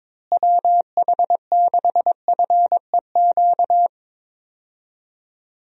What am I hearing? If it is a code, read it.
Morse code, heard as a keyed tone, about 22 wpm, 700 Hz. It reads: WH6FEQ
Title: USA Call Signs - 6 Characters - Repeated 2x - 22wpm